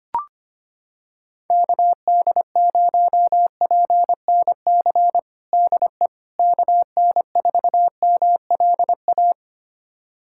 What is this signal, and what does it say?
Morse code, heard as a keyed tone, about 25 wpm, 700 Hz. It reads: KD0PNC DE KN4MLA
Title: USA Callsign DE Callsign - 6 Characters - 25wpm